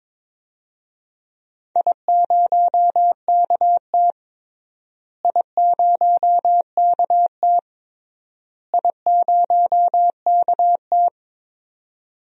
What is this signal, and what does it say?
Morse code, heard as a keyed tone, about 22 wpm, 700 Hz. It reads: I0KT I0KT I0KT
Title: DXCC-100 Call Signs - 4 Characters - Repeated 3x - 22wpm